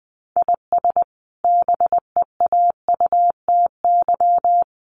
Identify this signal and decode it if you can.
Morse code, heard as a keyed tone, about 20 wpm, 700 Hz. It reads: IS BEAUTY
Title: Sentences from Top 1300 Words 20wpm